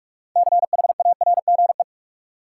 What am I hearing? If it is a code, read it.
Morse code, heard as a keyed tone, about 46 wpm, 700 Hz. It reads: CHARGE
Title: Sentences from Top 900 Words - New Word Drill - 45wpm